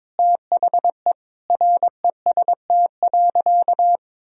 Morse code, heard as a keyed tone, about 22 wpm, 700 Hz. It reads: THE REST.
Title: Sentences from Top 1000 Words - Review Part 1 - 22wpm